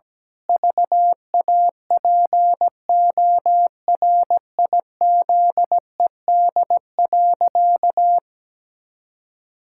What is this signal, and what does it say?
Morse code, heard as a keyed tone, about 17 wpm, 700 Hz. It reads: VAPORIZED.
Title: Pangram Challenge v2 - 17wpm